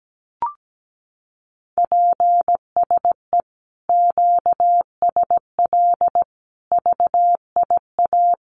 Morse code, heard as a keyed tone, about 17 wpm, 700 Hz. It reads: PSE QSL VIA
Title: Common QSO Phrases Encore 17wpm